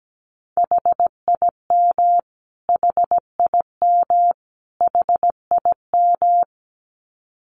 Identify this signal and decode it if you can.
Morse code, heard as a keyed tone, about 17 wpm, 700 Hz. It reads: HIM HIM HIM
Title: Top 200 Words Repeated 17wpm